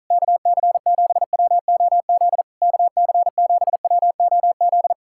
Morse code, heard as a keyed tone, about 46 wpm, 700 Hz. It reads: KC7WOZ KC7WOZ